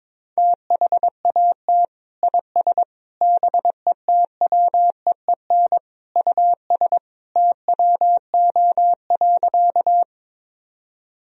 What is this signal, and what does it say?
Morse code, heard as a keyed tone, about 21 wpm, 700 Hz. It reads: THAT IS BETWEEN US TWO.
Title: Sentences from Top 1000 Words - Review Part 1 - 22wpm